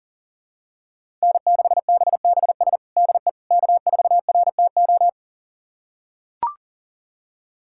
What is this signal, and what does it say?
Morse code, heard as a keyed tone, about 40 wpm, 700 Hz. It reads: N6BBS DE K4RTO
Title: USA Callsign DE Callsign - 5 Characters - 40wpm